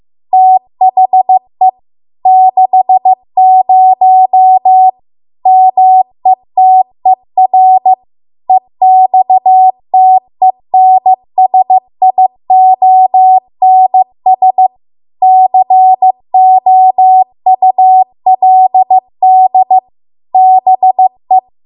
Morse code, heard as a keyed tone, about 15 wpm, 750 Hz. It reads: THE 60 METER EXTENSIONS COULD BE